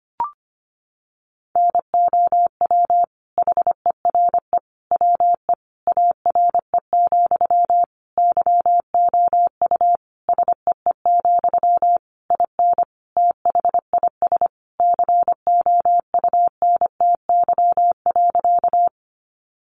Morse code, heard as a keyed tone, about 25 wpm, 700 Hz. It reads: NOW HERE WE ARE, YOU SEE, IN THIS COUNTY.
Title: Sentences from Top 1200 Words 25wpm